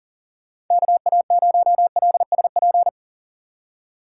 Morse code, heard as a keyed tone, about 37 wpm, 700 Hz. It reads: KA0LSP
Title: USA Call Signs - 6 Characters - 4x Character Spacing - 40wpm